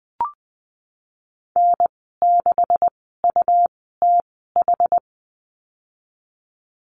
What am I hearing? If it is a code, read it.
Morse code, heard as a keyed tone, about 20 wpm, spaced as for 14 wpm, 700 Hz. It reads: N6UTH
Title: 128 Lesson 21g - Number 6 - Call Signs 20wpm